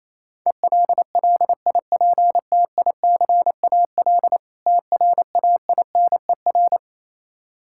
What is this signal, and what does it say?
Morse code, heard as a keyed tone, about 27 wpm, 700 Hz. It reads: ELLIPTICAL TRAINER
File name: Easy Words - Speed-Racing with Hints - 22wpm